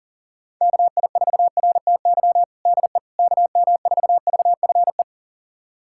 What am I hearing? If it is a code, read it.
Morse code, heard as a keyed tone, about 40 wpm, 700 Hz. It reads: KI4RTY DE KM4VFE
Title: USA Callsign DE Callsign - 6 Characters - 40wpm